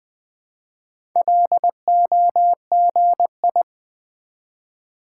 Morse code, heard as a keyed tone, about 20 wpm, 700 Hz. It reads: LOGI
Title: Build A Word - 1 Letter At A Time - 5-Letter Words - 20wpm